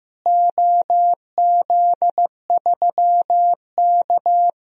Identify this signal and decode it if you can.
Morse code, heard as a keyed tone, about 15 wpm, 700 Hz. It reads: OZ3K
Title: DXCC-100 Call Signs - 4 Characters - Repeated 3x - 15wpm